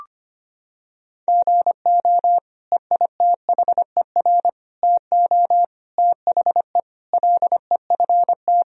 Morse code, heard as a keyed tone, about 25 wpm, 700 Hz. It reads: GO EITHER TO THE LEFT
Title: Sentences from Top 1000 Words - Review Part 4 - 25wpm